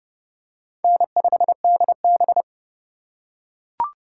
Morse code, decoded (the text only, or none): N5DB